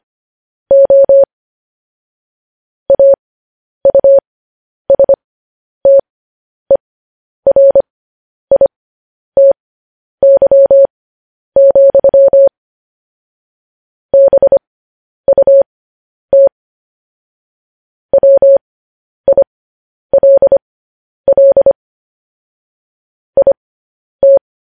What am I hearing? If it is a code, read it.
Morse code, heard as a keyed tone, about 25 wpm, spaced as for 10 wpm, 550 Hz. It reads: O AUSTERITY, BUT WILL IT
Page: news-headlines-in-morse-code-at-10-wpm